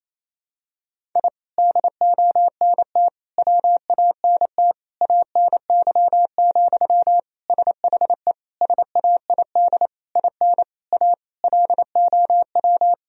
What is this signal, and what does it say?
Morse code, heard as a keyed tone, about 28 wpm, 700 Hz. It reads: I DONT WANT ANY, SHE SAID IN A LOW
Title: Sentences from Top 1000 Words - Review Part 1 - 28wpm